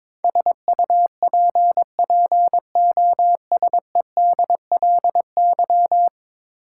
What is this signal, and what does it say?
Morse code, heard as a keyed tone, about 22 wpm, 700 Hz. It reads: SUPPOSEDLY
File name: practice / 22wpm / 10 Letter Words - First 3 Letters Given - 22wpm